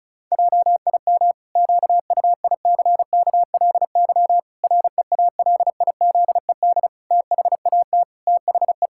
Morse code, heard as a keyed tone, about 36 wpm, 700 Hz. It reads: JIM QUICKLY REALIZED THAT THE
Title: Pangram Challenge v2 - 35wpm